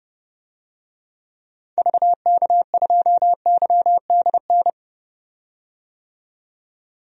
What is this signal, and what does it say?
Morse code, heard as a keyed tone, about 31 wpm, 700 Hz. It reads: VK2YDN
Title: DXCC-100 Call Signs - 6 Characters - 30wpm